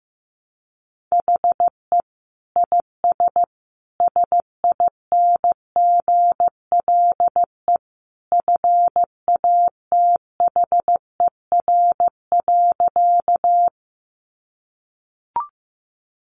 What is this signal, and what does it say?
Morse code, heard as a keyed tone, about 15 wpm, 700 Hz. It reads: HE IS SINGLE FATHER.
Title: Sentences from Top 700 Words 15wpm